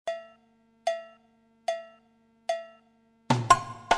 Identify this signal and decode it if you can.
There is no code.